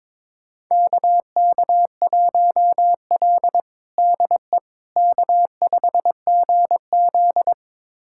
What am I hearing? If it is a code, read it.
Morse code, heard as a keyed tone, about 22 wpm, 700 Hz. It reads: KK1L DE K5GZ
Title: USA Callsign DE Callsign - 4 Characters - 22wpm